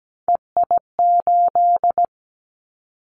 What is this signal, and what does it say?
Morse code, heard as a keyed tone, about 17 wpm, 700 Hz. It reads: EI8
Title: Callsign Prefixes for Top 50 DXCC - 17wpm